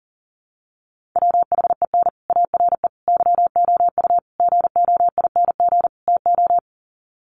Morse code, heard as a keyed tone, about 40 wpm, 700 Hz. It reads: WHEN ARE YOU GOING TO
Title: Sets of 5 Words Encore2 40wpm